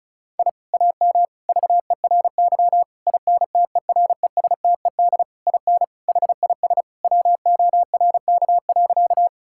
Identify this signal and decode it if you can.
Morse code, heard as a keyed tone, about 34 wpm, 700 Hz. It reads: I AM VERY INTERESTED IN HIS WORK.
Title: Sentences from Top 1000 Words - Review Part 6 - 35wpm